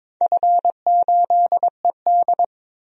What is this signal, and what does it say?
Morse code, heard as a keyed tone, about 22 wpm, 700 Hz. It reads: F8ED
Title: DXCC-100 Call Signs - 4 Characters - 22wpm